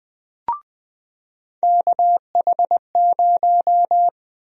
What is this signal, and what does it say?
Morse code, heard as a keyed tone, about 21 wpm, 700 Hz. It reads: KH0
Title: USA Call Sign Prefixes - 20wpm